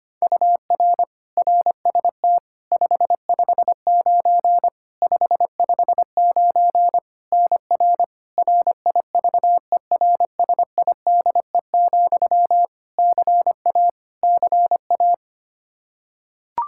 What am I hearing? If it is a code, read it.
Morse code, heard as a keyed tone, about 25 wpm, 700 Hz. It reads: UR RST 559 559 NR RIVERSIDE, CA CA
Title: Practice QSO - Exchange 1 - Beginning - 25wpm